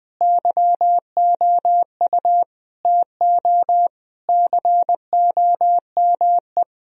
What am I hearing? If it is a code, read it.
Morse code, heard as a keyed tone, about 19 wpm, 700 Hz. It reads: YOU TO COME
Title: Sentences from Top 1000 Words - Review Part 3 - 20wpm